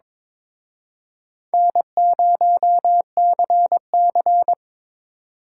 Morse code, heard as a keyed tone, about 26 wpm, 700 Hz. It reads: N0CC
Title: Speed Racing - USA Call Signs 22wpm